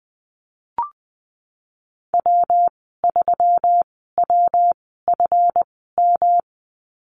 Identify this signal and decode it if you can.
Morse code, heard as a keyed tone, about 19 wpm, spaced as for 14 wpm, 700 Hz. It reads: W3WFM
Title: 173 Lesson 28g - Number 9 - Call Signs 20wpm